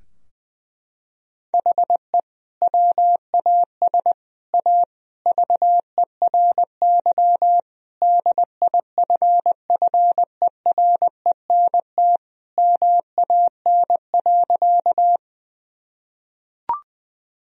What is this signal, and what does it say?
Morse code, heard as a keyed tone, about 20 wpm, 700 Hz. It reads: HE WAS A VERY DIFFERENT MAN.
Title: Sentences from Top 1000 Words - Review Part 3 - 20wpm